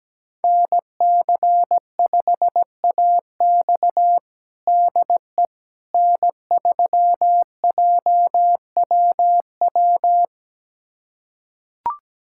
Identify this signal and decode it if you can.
Morse code, heard as a keyed tone, about 17 wpm, 700 Hz. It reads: NC5AX DE N3JWW
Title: USA Callsign DE Callsign - 5 Characters - 17wpm